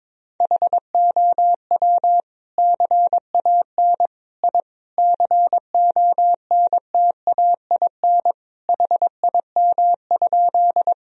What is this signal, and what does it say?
Morse code, heard as a keyed tone, about 22 wpm, 700 Hz. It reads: HOW CAN I CONTAIN HIM?
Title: Sentences from Top 1000 Words - Review Part 2 - 22wpm